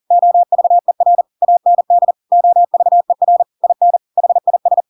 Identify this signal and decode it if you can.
Morse code, heard as a keyed tone, about 40 wpm, 700 Hz. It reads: OVER AND OVER IN HIS
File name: Pinocchio / Pinocchio ch5 40 40 ik1hge